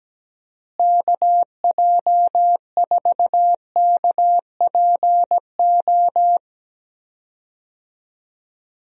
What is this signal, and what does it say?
Morse code, heard as a keyed tone, about 17 wpm, 700 Hz. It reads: KJ4KPO
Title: USA Call Signs - 6 Characters - Repeated 2x - 17wpm